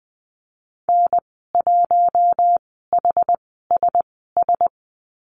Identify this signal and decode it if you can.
Morse code, heard as a keyed tone, about 20 wpm, spaced as for 14 wpm, 700 Hz. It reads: N1HSS
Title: 076 Lesson 13g - Letter L - Call Signs 20wpm